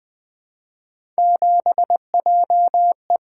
Morse code, heard as a keyed tone, about 20 wpm, 700 Hz. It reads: 7JE
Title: Three Letter-Number Combinations - 20wpm